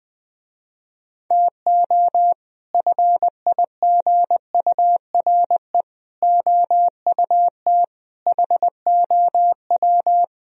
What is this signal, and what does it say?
Morse code, heard as a keyed tone, about 21 wpm, 700 Hz. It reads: TO FIGURE OUT HOW
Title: Speed-Racing - Sets of 5 Words 20wpm